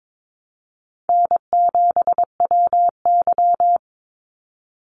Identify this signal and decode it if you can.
Morse code, heard as a keyed tone, about 22 wpm, 700 Hz. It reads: N7WY
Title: Easy Call Signs Repeated 3x - 22wpm